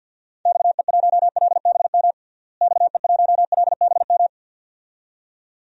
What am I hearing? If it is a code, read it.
Morse code, heard as a keyed tone, about 52 wpm, 700 Hz. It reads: XE1LBM XE1LBM
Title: DXCC-100 Call Signs - 6 Characters - Repeated 2x - 50wpm